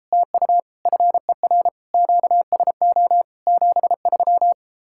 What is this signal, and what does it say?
Morse code, heard as a keyed tone, about 33 wpm, 700 Hz. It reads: TU FER QSO 73